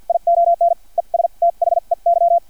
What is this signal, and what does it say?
Morse code, heard as a keyed tone, about 50 wpm, spaced as for 41 wpm, 680 Hz, with radio noise. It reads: ION ES THEY
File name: archived / CRSnet